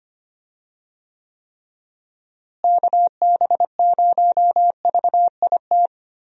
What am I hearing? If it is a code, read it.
Morse code, heard as a keyed tone, about 25 wpm, 700 Hz. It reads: KB0VIT
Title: USA Call Signs - 6 Characters - Repeated 3x - 25wpm